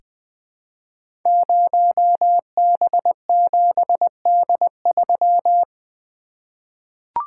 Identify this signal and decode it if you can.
Morse code, heard as a keyed tone, about 20 wpm, 700 Hz. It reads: 0B7D3
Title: Code Groups - 5 Letter:Numbers - 20wpm